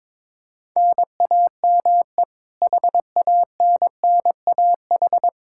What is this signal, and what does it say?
Morse code, heard as a keyed tone, about 22 wpm, 700 Hz. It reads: NAME HANNAH
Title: Practice QSO - Exchange 1 - Beginning - 22wpm